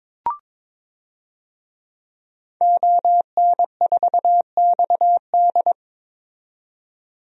Morse code, heard as a keyed tone, about 22 wpm, 700 Hz. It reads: ON4XD
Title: DXCC-100 Call Signs - 5 Characters - 22wpm